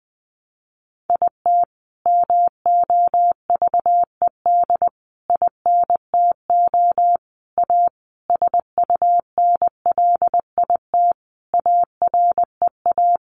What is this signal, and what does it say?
Morse code, heard as a keyed tone, about 20 wpm, 700 Hz. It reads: IT MOVED INTO A SUNLIT AREA
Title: A Sound Of Thunder 20wpm